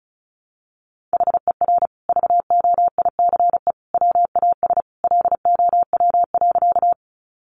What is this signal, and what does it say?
Morse code, heard as a keyed tone, about 34 wpm, 700 Hz. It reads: HER VOICE WAS LOW.
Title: Sentences from Top 500 Words 35wpm